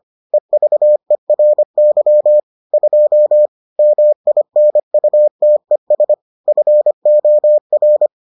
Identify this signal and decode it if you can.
Morse code, heard as a keyed tone, about 25 wpm, 600 Hz. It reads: EVERY 2 MINUTES FOR